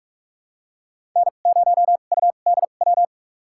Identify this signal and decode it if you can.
Morse code, heard as a keyed tone, about 43 wpm, spaced as for 32 wpm, 700 Hz. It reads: N0UDW
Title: USA Call Signs - 5 Characters - 2x Character Spacing - 45wpm